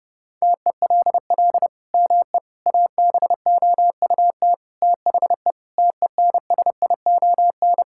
Morse code, heard as a keyed tone, about 30 wpm, 700 Hz. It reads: TELL ME ABOUT THE TENSION
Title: Sentences from Top 1600 Words 30wpm